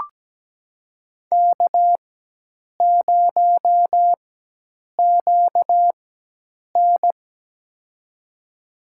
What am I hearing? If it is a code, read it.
Morse code, heard as a keyed tone, about 17 wpm, spaced as for 8 wpm, 700 Hz. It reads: K0QN
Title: USA Call Signs - 4 Characters - 4x Character Spacing - 17wpm